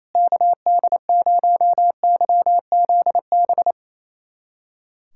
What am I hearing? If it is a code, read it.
Morse code, heard as a keyed tone, about 28 wpm, 700 Hz. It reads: KD0YZB
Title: USA Callsign DE Callsign - 6 Characters - 28wpm